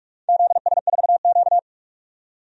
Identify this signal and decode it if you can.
Morse code, heard as a keyed tone, about 44 wpm, 700 Hz. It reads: ZS4Q
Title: DXCC-100 Call Signs - 4 Characters - 45wpm